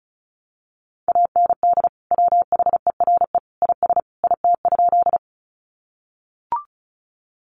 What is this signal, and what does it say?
Morse code, heard as a keyed tone, about 34 wpm, 700 Hz. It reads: AND WHERE IS IT?
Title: Sentences from Top 200 Words 35wpm